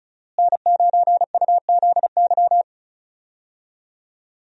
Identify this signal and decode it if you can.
Morse code, heard as a keyed tone, about 34 wpm, 700 Hz. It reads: N9UZY
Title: USA Call Signs - 5 Characters - Repeated 3x - 35wpm